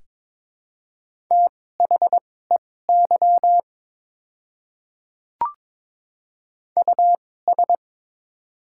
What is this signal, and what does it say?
Morse code, heard as a keyed tone, about 22 wpm, spaced as for 16 wpm, 700 Hz. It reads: THEY US
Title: 2x Character Spacing - Top 300 Words - 22wpm